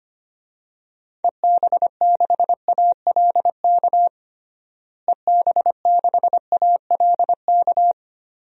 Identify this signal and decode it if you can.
Morse code, heard as a keyed tone, about 25 wpm, 700 Hz. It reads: EB6ALK EB6ALK
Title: DXCC-100 Call Signs - 6 Characters - Repeated 3x - 25wpm